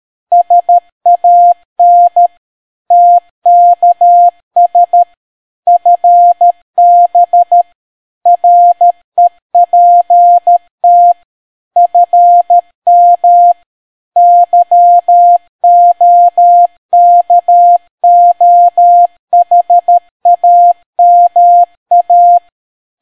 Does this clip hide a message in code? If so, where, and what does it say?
Yes: Morse code from 0.3 to 22.4 s, heard as a keyed tone, about 13 wpm, 700 Hz. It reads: SAN TKS FB REPT FM YOKOHAMA